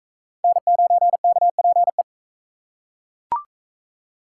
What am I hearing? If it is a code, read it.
Morse code, heard as a keyed tone, about 41 wpm, 700 Hz. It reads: N9KPE